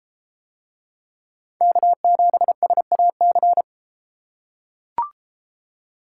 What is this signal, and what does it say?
Morse code, heard as a keyed tone, about 31 wpm, 700 Hz. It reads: K7SAC